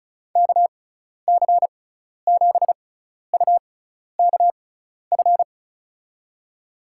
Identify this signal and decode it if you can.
Morse code, heard as a keyed tone, about 36 wpm, spaced as for 12 wpm, 690 Hz. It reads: KC7UKF
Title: USA Call Signs - 6 Characters - 6x Character Spacing - 35wpm